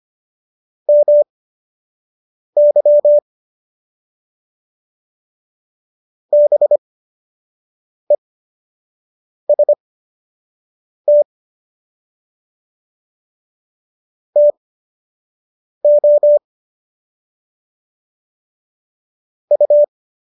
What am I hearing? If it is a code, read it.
Morse code, heard as a keyed tone, about 25 wpm, spaced as for 6 wpm, 590 Hz. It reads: MY BEST TO U
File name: qso012 06